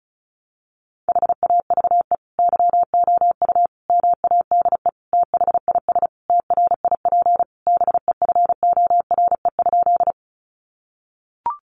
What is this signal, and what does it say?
Morse code, heard as a keyed tone, about 34 wpm, 700 Hz. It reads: HAVE YOU MADE THIS TRIP BEFORE?
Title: Sentences from Top 1300 Words 35wpm